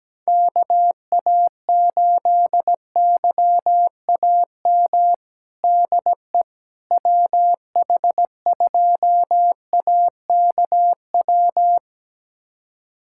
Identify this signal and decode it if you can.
Morse code, heard as a keyed tone, about 17 wpm, 700 Hz. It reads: KA8YAM DE WH2AKW
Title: USA Callsign DE Callsign - 6 Characters - 17wpm